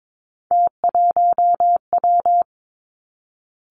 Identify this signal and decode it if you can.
Morse code, heard as a keyed tone, about 22 wpm, 700 Hz. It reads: T1W
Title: Three Challenging Letter-Number Combinations 22wpm